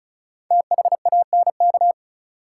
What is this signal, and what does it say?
Morse code, heard as a keyed tone, about 36 wpm, 700 Hz. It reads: THANK
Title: Sets of 2 Words - New Word Drill - 35wpm